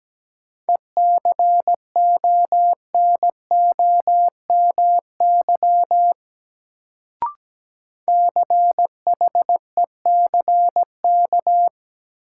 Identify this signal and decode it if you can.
Morse code, heard as a keyed tone, about 17 wpm, 700 Hz. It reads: ECONOMY CHECK
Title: Sentences from Top 1000 Words - Review Part 3 - New Word Drill - 17wpm